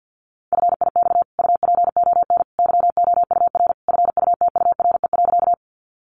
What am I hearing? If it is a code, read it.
Morse code, heard as a keyed tone, about 50 wpm, 700 Hz. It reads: FIX UPON YOUR FUTURE.
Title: Sentences from Top 1400 Words 50wpm